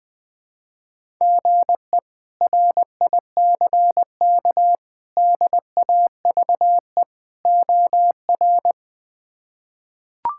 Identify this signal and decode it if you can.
Morse code, heard as a keyed tone, about 21 wpm, 700 Hz. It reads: GE RICK DAVE OR
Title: K1USN SST QSO Practice - 20wpm